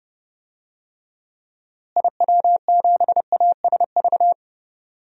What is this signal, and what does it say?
Morse code, heard as a keyed tone, about 30 wpm, 700 Hz. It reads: IW7ASV
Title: DXCC-100 Call Signs - 6 Characters - Repeated 2x - 30wpm